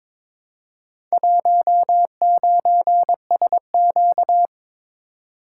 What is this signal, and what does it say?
Morse code, heard as a keyed tone, about 22 wpm, 700 Hz. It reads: 19SQ
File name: practice / 22wpm / Four Letter-Number Combinations - 22wpm